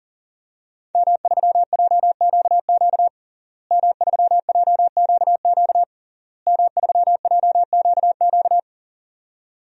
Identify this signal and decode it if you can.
Morse code, heard as a keyed tone, about 40 wpm, 700 Hz. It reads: M3JQQ M3JQQ M3JQQ
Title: DXCC-100 Call Signs - 5 Characters - Repeated 3x - 40wpm